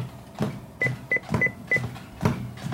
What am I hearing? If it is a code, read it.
There is no code.